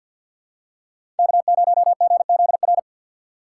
Morse code, heard as a keyed tone, about 50 wpm, 700 Hz. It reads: K0GZR
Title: USA Call Signs - 5 Characters - 3x Character Spacing - 50wpm